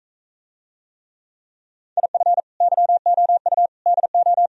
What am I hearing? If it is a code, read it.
Morse code, heard as a keyed tone, about 34 wpm, 700 Hz. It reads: IF YOU DO
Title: Speed-Racing - Sets of 5 Words 28wpm